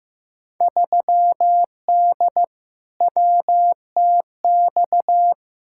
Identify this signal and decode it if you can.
Morse code, heard as a keyed tone, about 16 wpm, 700 Hz. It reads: 3D WTX
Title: Field Day Exchanges Encore - 15wpm